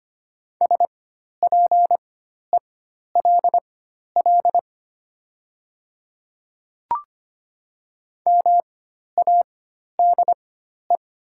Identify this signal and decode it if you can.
Morse code, heard as a keyed tone, about 25 wpm, spaced as for 12 wpm, 700 Hz. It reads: SPELL MADE
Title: 4x Character Spacing - Top 200 Words - 25wpm